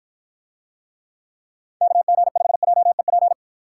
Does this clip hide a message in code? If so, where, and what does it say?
Yes: Morse code from 1.8 to 3.3 s, heard as a keyed tone, about 53 wpm, 700 Hz. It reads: KG5JEP